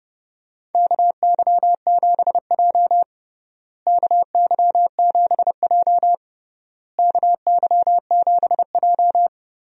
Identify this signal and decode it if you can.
Morse code, heard as a keyed tone, about 30 wpm, 700 Hz. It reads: KY7J KY7J KY7J
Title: USA Call Signs - 4 Characters - Repeated 3x - 30wpm